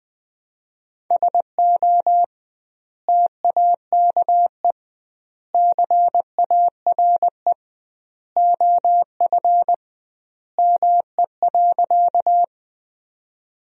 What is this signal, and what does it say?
Morse code, heard as a keyed tone, about 20 wpm, 700 Hz. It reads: SO TAKE CARE OF ME.
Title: 2x Word-Spacing - Sentences from Top 300 Words - 20wpm